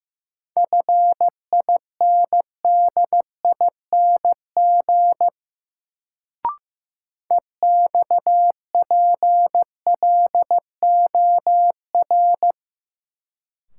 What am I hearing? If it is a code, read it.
Morse code, heard as a keyed tone, about 15 wpm, 700 Hz. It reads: FINDING EXPLOR
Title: Sentences from Top 1100 Words - New Word Drill - 15wpm